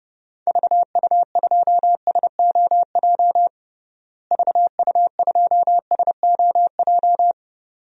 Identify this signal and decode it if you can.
Morse code, heard as a keyed tone, about 30 wpm, 700 Hz. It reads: VU2SOJ VU2SOJ
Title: DXCC-100 Call Signs - 6 Characters - Repeated 2x - 30wpm